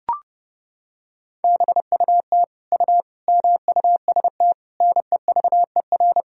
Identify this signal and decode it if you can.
Morse code, heard as a keyed tone, about 30 wpm, 700 Hz. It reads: BUT U MUST NEVER
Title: Sentences from Top 300 Words and Abbreviations 30wpm